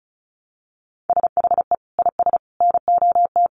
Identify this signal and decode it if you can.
Morse code, heard as a keyed tone, about 34 wpm, 700 Hz. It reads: SHE IS NOT
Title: Sentences from Top 1100 Words 35wpm